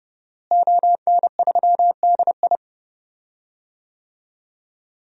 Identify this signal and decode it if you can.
Morse code, heard as a keyed tone, about 30 wpm, 700 Hz. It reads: ON3DI
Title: DXCC-100 Call Signs - 5 Characters - Repeated 3x - 30wpm